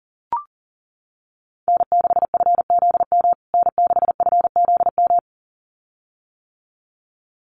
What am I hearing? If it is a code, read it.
Morse code, heard as a keyed tone, about 40 wpm, 700 Hz. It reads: N6FZM N6FZM
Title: USA Call Signs Repeated 2x 40wpm